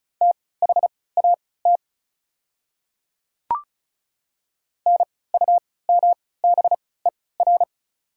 Morse code, heard as a keyed tone, about 36 wpm, spaced as for 20 wpm, 690 Hz. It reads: THAT NUMBER
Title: 3x Character Spacing - Top 200 Words - 35wpm